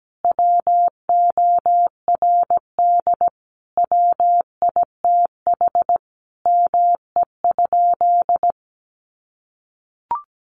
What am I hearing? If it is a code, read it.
Morse code, heard as a keyed tone, about 17 wpm, 700 Hz. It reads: WORD WITH ME?
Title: Sentences from Top 300 Words 17wpm